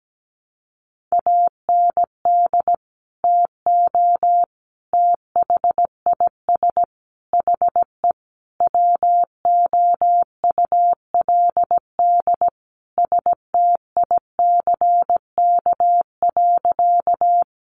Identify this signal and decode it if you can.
Morse code, heard as a keyed tone, about 17 wpm, 700 Hz. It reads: AND TO THIS HE WOULD STICK.
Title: Sentences from Top 1200 Words 17wpm